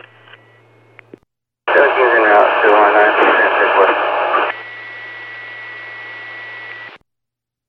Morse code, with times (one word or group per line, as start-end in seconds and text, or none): none